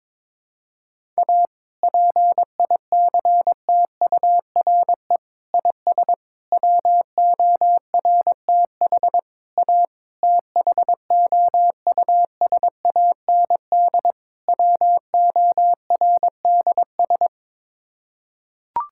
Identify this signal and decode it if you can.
Morse code, heard as a keyed tone, about 21 wpm, 700 Hz. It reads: A PICTURE IS WORTH A THOUSAND WORDS
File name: practice / 22wpm / Sets of 7 Words - 22wpm